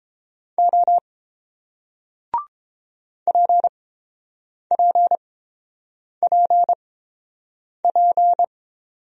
Morse code, heard as a keyed tone, about 30 wpm, spaced as for 7 wpm, 700 Hz. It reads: O PPPP